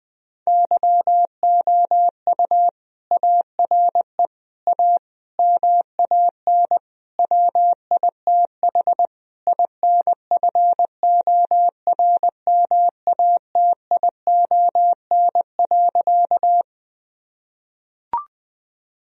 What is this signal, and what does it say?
Morse code, heard as a keyed tone, about 20 wpm, 700 Hz. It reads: YOU ARE A MAN WITH INFORMATION.
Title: Sentences from Top 1000 Words - Review Part 3 - 20wpm